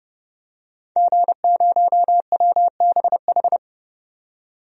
Morse code, heard as a keyed tone, about 32 wpm, 700 Hz. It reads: G0WBH
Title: DXCC-100 Call Signs - 5 Characters - 30wpm